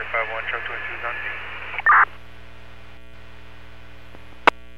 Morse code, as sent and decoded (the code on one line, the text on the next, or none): none